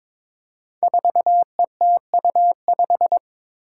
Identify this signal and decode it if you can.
Morse code, heard as a keyed tone, about 22 wpm, 700 Hz. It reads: 4ETU5
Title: Five Challenging Letter-Number Combinations 22wpm